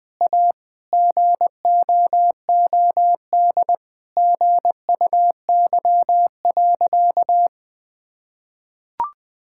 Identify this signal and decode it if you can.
Morse code, heard as a keyed tone, about 20 wpm, 700 Hz. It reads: A GOOD GUY.
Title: Sentences with 3-Word Phrases - Spoken Beginning - 20wpm